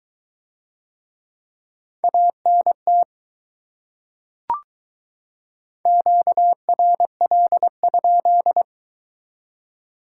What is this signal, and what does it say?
Morse code, heard as a keyed tone, about 22 wpm, 700 Hz. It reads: ANT QRL?